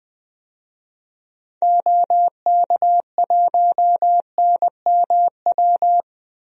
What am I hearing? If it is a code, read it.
Morse code, heard as a keyed tone, about 20 wpm, 700 Hz. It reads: OK1NMW
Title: DXCC-100 Call Signs - 6 Characters - Repeated 3x - 20wpm